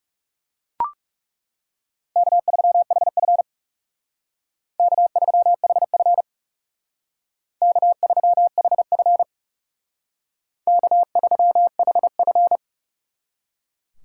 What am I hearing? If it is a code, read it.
Morse code, heard as a keyed tone, about 37 wpm, 700 Hz. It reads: K3HF K3HF K3HF K3HF
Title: Speed Racing - USA Call Signs 30wpm